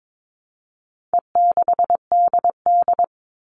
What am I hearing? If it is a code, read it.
Morse code, heard as a keyed tone, about 22 wpm, 700 Hz. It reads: E6DD
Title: All Cut Numbers - 4-Digit 22wpm